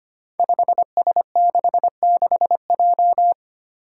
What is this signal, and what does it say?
Morse code, heard as a keyed tone, about 25 wpm, 700 Hz. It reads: 5S66J
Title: Five Challenging Letter-Number Combinations 25wpm